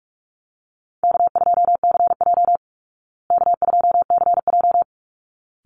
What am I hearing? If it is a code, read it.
Morse code, heard as a keyed tone, about 44 wpm, 700 Hz. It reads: K2CJ K2CJ
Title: Easy Call Signs Repeated 2x - 45wpm